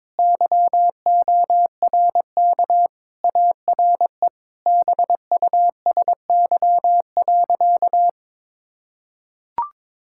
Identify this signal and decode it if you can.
Morse code, heard as a keyed tone, about 22 wpm, 700 Hz. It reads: YORK ARE BUSY.
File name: practice / 22wpm / Sentences with 3-Word Phrases - 22wpm